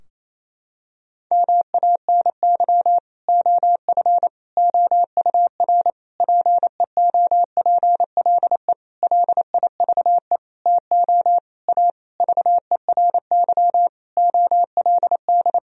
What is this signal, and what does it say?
Morse code, heard as a keyed tone, about 28 wpm, 700 Hz. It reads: MANY OF OUR PEOPLE LIVE TO A VERY OLD
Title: Sentences from Top 1000 Words - Review Part 3 - 28wpm